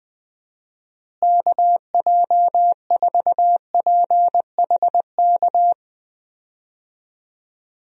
Morse code, heard as a keyed tone, about 20 wpm, 700 Hz. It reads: KJ4PHK
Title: USA Call Signs - 6 Characters - Repeated 3x - 20wpm